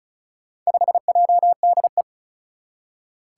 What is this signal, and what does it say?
Morse code, heard as a keyed tone, about 36 wpm, 700 Hz. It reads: 5JDE
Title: Four Challenging Letter-Number Combinations 35wpm